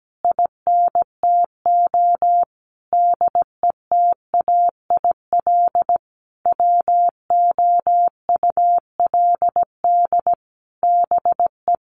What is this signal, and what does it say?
Morse code, heard as a keyed tone, about 17 wpm, 700 Hz. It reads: INTO DETAIL WOULD BE
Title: Sentences from Top 1500 Words 17wpm